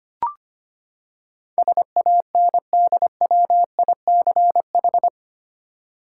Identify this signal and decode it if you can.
Morse code, heard as a keyed tone, about 25 wpm, 700 Hz. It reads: SANDWICH
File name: Eight Letter Words - 25wpm